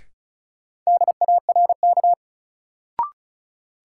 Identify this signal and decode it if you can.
Morse code, heard as a keyed tone, about 36 wpm, 700 Hz. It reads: DARK
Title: Sentences from Top 500 Words - New Word Drill - 35wpm